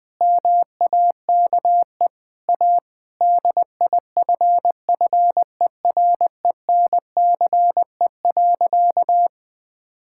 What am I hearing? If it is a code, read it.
Morse code, heard as a keyed tone, about 20 wpm, 700 Hz. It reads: MAKE A DIFFERENCE.
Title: Sentences with 3-Word Phrases - 20wpm